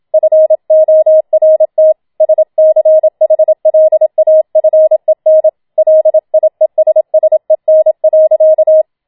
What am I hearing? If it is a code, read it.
Morse code, heard as a keyed tone, about 27 wpm, 630 Hz. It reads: FORT SCHLAFEN LIESSEN.
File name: Schneewittchen Teil2 27wpm